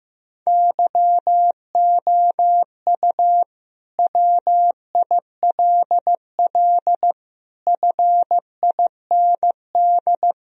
Morse code, heard as a keyed tone, about 15 wpm, 700 Hz. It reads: YOU WILL FIND
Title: Sentences from Top 1000 Words - Review Part 4 - 15wpm